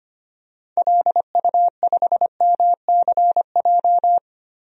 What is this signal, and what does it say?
Morse code, heard as a keyed tone, about 25 wpm, 700 Hz. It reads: LU5MCJ
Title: DXCC-100 Call Signs - 6 Characters - 25wpm